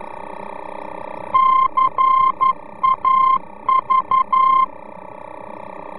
Morse code, heard as a keyed tone, about 11 wpm, 1050 Hz. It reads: CAV
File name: france / mx / CAV-VOR-F